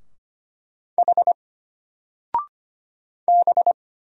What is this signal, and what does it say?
Morse code, heard as a keyed tone, about 25 wpm, 700 Hz. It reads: H B
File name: Single-Letter-Number-25wpm